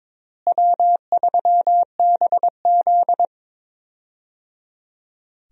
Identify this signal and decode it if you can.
Morse code, heard as a keyed tone, about 22 wpm, 700 Hz. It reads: W3BZ
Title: USA Call Signs - 4 Characters - Repeated 3x - 22wpm